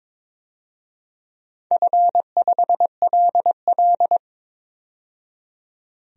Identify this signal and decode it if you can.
Morse code, heard as a keyed tone, about 22 wpm, 700 Hz. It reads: F5LL
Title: DXCC-100 Call Signs - 4 Characters - 22wpm